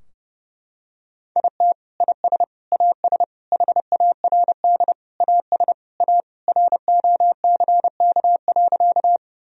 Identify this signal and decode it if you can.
Morse code, heard as a keyed tone, about 30 wpm, 700 Hz. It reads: IT IS AS HARD AS A ROCK.
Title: Sentences from Top 1000 Words - Review Part 2 - 30wpm